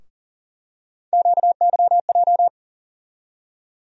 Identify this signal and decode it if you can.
Morse code, heard as a keyed tone, about 40 wpm, 700 Hz. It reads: QYJ
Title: USA Call Sign Postfixes - 3 Letters 40wpm